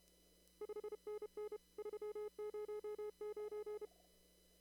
Morse code, heard as a keyed tone, about 30 wpm, 420 Hz. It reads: 5NN 309
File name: CW / kt5j wpx cw-op-k5pi